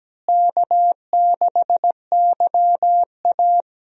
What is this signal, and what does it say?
Morse code, heard as a keyed tone, about 17 wpm, 700 Hz. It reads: K6YA
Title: USA Call Signs - 4 Characters - Repeated 2x - 17wpm